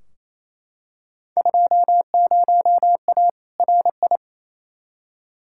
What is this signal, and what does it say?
Morse code, heard as a keyed tone, about 28 wpm, 700 Hz. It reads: 20A RI
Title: Field Day Exchanges Encore - 28wpm